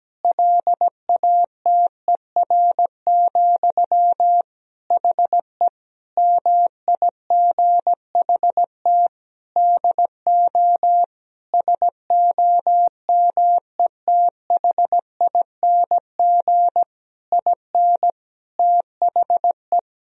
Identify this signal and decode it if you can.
Morse code, heard as a keyed tone, about 17 wpm, 700 Hz. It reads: LATER, HE MIGHT DO SOMETHING IN THE
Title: Sentences from Top 1000 Words - Review Part 4 - 17wpm